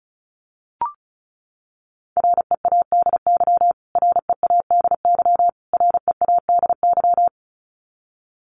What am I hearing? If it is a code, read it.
Morse code, heard as a keyed tone, about 34 wpm, 700 Hz. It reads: READY READY READY
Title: Top 500 Words Repeated 35wpm